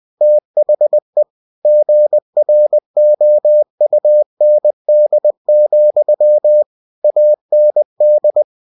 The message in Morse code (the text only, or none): THE GROUND, AND